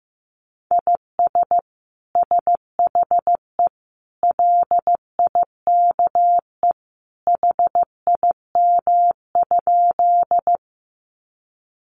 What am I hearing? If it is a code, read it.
Morse code, heard as a keyed tone, about 15 wpm, 700 Hz. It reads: IS SHE LIKE HIM?
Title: Sentences from Top 100 Words 15wpm